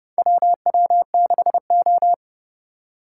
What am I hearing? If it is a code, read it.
Morse code, heard as a keyed tone, about 30 wpm, 700 Hz. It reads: WW6O
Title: USA Call Signs - 4 Characters - 3x Character Spacing - 30wpm